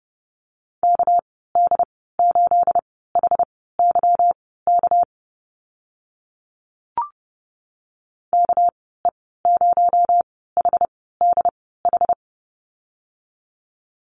Morse code, heard as a keyed tone, about 30 wpm, spaced as for 17 wpm, 700 Hz. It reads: KD8HYK KE0HDH
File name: Lesson 37g - Letter x - Call Signs 30wpm